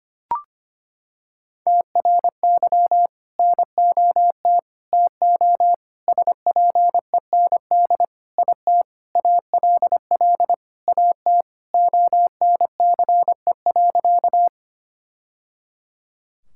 Morse code, heard as a keyed tone, about 25 wpm, 700 Hz. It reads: TRY NOT TO SPEND IT ALL AT ONCE.
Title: Sentences from Top 1000 Words - Review Part 5 - 25wpm